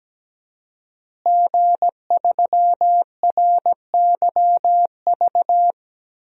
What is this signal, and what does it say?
Morse code, heard as a keyed tone, about 17 wpm, 700 Hz. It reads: G3RYV
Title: DXCC-100 Call Signs - 5 Characters - Repeated 2x - 17wpm